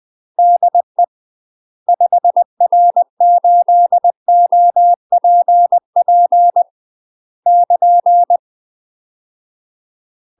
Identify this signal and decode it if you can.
Morse code, heard as a keyed tone, about 20 wpm, 700 Hz. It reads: DE 5R8OPP <KN>
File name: UK MASTER QSOs TRAINING / 20WPM / UK MASTER QSOs 0012